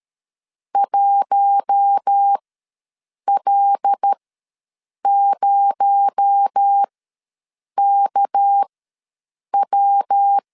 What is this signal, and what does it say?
Morse code, heard as a keyed tone, about 13 wpm, spaced as for 7 wpm, 800 Hz. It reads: 1L0KW
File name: LEZIONI / 6 PKX0 / 6 PKX0 ws 12wpm 3448